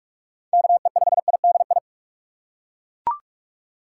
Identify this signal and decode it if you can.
Morse code, heard as a keyed tone, about 42 wpm, 700 Hz. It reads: KE5IDI
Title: USA Call Signs - 6 Characters - 6x Character Spacing - 45wpm